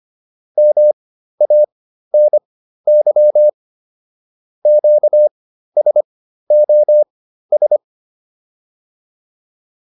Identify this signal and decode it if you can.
Morse code, heard as a keyed tone, about 25 wpm, spaced as for 13 wpm, 600 Hz. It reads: MANY QSOS